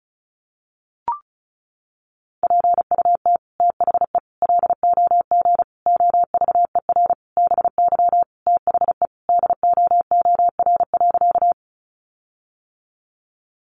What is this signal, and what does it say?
Morse code, heard as a keyed tone, about 34 wpm, 700 Hz. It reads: PUT THE LOG OVER BY THE DOOR.
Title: Sentences from Top 1400 Words 35wpm